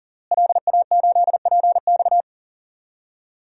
Morse code, heard as a keyed tone, about 40 wpm, 700 Hz. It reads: LA8PX
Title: DXCC-100 Call Signs - 5 Characters - Repeated 3x - 40wpm